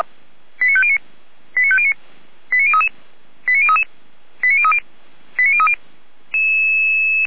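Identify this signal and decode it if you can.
Five-tone selective call (ZVEI), heard as quick runs of five tones, tones of about 70 ms. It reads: 89509 89509 89033 89033 89030 89030
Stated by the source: ZVEI five-tone selective call received by SDR, address 89001, from Feuerwehr Orlamünde fire brigade server.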